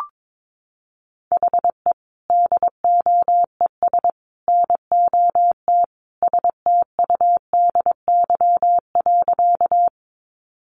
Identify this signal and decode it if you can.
Morse code, heard as a keyed tone, about 22 wpm, 700 Hz. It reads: HE DOES NOT STUDY.
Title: Additional Easy Phrases 22wpm